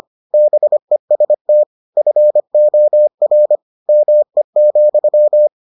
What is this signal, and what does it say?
Morse code, heard as a keyed tone, about 25 wpm, 600 Hz. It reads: BEST FOR ME,